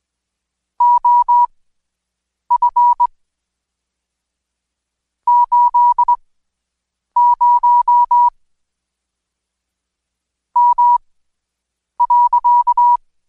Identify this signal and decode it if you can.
Morse code, heard as a keyed tone, about 21 wpm, spaced as for 7 wpm, 970 Hz. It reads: OF 80 M.
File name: Code Quick / AudioDownld2 / Seg113-Day18-DayDrill 18